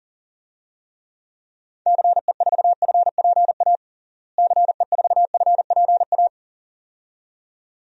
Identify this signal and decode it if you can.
Morse code, heard as a keyed tone, about 40 wpm, 700 Hz. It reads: CE4FPA CE4FPA
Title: DXCC-100 Call Signs - 6 Characters - Repeated 2x - 40wpm